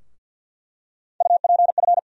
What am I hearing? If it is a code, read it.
Morse code, heard as a keyed tone, about 48 wpm, 700 Hz. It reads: UPF
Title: USA Call Sign Postfixes - 3 Letters 50wpm